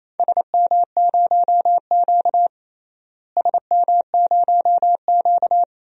Morse code, heard as a keyed tone, about 28 wpm, 700 Hz. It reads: SM0Q SM0Q
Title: DXCC-100 Call Signs - 4 Characters - Repeated 2x - 28wpm